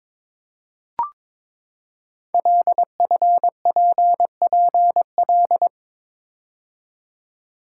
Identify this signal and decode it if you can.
Morse code, heard as a keyed tone, about 22 wpm, 700 Hz. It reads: LFPPL
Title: Code Groups - 5 Letters - 22wpm